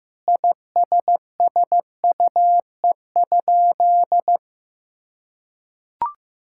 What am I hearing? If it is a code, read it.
Morse code, heard as a keyed tone, about 15 wpm, 700 Hz. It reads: ISSUE?
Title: Sentences from Top 1000 Words - Review Part 6 - 15wpm